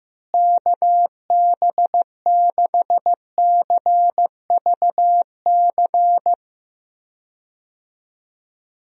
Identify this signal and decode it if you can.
Morse code, heard as a keyed tone, about 15 wpm, 700 Hz. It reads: KB6CVC
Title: USA Call Signs - 6 Characters - Repeated 2x - 15wpm